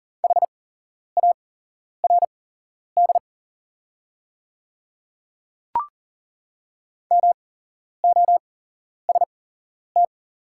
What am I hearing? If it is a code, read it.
Morse code, heard as a keyed tone, about 40 wpm, spaced as for 11 wpm, 700 Hz. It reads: HARD MOST
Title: 8x Character Spacing - Top 300 Words - 40wpm